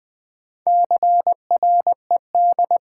Morse code, heard as a keyed tone, about 20 wpm, 700 Hz. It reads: CRED
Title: Build A Word - 2 Letters At A Time - 8-Letter Words - 20wpm